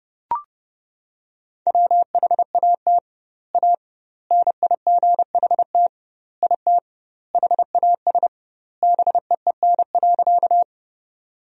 Morse code, heard as a keyed tone, about 30 wpm, 700 Hz. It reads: WHAT A NIGHT IT HAS BEEN.
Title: 2x Word-Spacing - Sentences from Top 300 Words - 30wpm